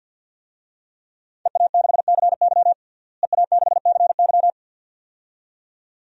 Morse code, heard as a keyed tone, about 50 wpm, 700 Hz. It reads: EA6CY EA6CY
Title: DXCC-100 Call Signs - 5 Characters - Repeated 2x - 50wpm